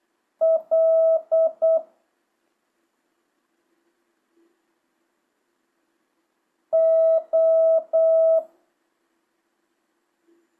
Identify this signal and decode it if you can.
Morse code, heard as a keyed tone, about 8 wpm, 1300 Hz. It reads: L O
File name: jh8cbh / 2025cw